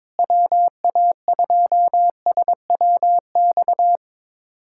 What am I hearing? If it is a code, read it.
Morse code, heard as a keyed tone, about 22 wpm, 700 Hz. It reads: WA2SWX
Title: K1USN SST QSO Practice - 22wpm